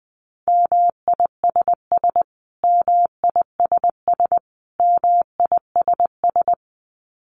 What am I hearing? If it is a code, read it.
Morse code, heard as a keyed tone, about 20 wpm, 700 Hz. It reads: MISS MISS MISS
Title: Top 500 Words Repeated 20wpm